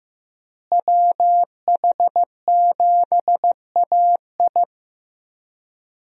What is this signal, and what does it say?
Morse code, heard as a keyed tone, about 15 wpm, 700 Hz. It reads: WH7AI
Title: USA Call Signs - 5 Characters - Repeated 2x - 15wpm